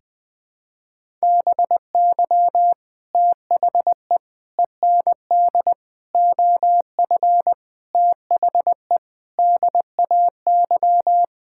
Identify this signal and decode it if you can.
Morse code, heard as a keyed tone, about 20 wpm, 700 Hz. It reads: BY THE END OF THE DAY
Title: Sentences with 4-Word Phrases - 20wpm